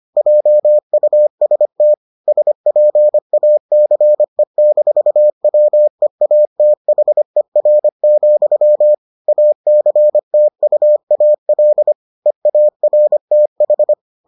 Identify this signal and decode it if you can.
Morse code, heard as a keyed tone, about 25 wpm, 600 Hz. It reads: JUST SPACE-WEATHER, ACTUAL EARTH